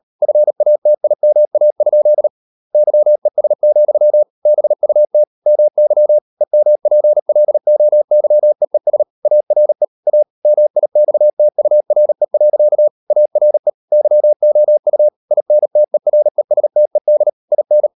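Morse code, heard as a keyed tone, about 38 wpm, 600 Hz. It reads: FATIMA? YES, BUT MY EMPLOYEES ARE A MIXTURE. ARE YOU INTERESTED IN
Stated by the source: Morse code practice at 38 WPM, sending Ken Follett's novel Never in English, from CW med Gnister.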